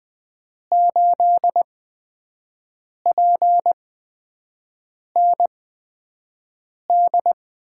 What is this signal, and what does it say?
Morse code, heard as a keyed tone, about 20 wpm, spaced as for 5 wpm, 700 Hz. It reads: 8PND
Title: USA Call Signs - 6 Characters - 8x Character Spacing - 20wpm